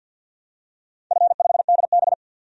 Morse code, heard as a keyed tone, about 52 wpm, 700 Hz. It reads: F5DB
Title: DXCC-100 Call Signs - 4 Characters - 50wpm